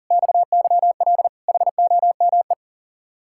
Morse code, heard as a keyed tone, about 50 wpm, 700 Hz. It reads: XYL HOME
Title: Speed-Racing Common QSO Phrases - 40wpm